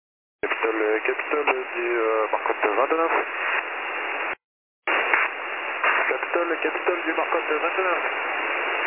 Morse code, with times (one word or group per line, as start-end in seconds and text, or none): none